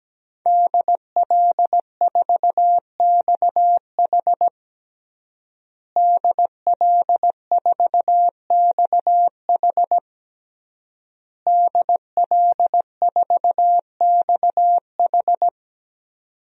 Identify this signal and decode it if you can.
Morse code, heard as a keyed tone, about 17 wpm, 700 Hz. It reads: DL4XH DL4XH DL4XH
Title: DXCC-100 Call Signs - 5 Characters - Repeated 3x - 17wpm